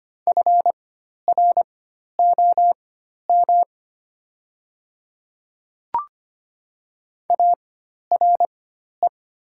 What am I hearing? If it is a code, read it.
Morse code, heard as a keyed tone, about 25 wpm, spaced as for 12 wpm, 700 Hz. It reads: FROM ARE
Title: 4x Character Spacing - Top 100 Words - 25wpm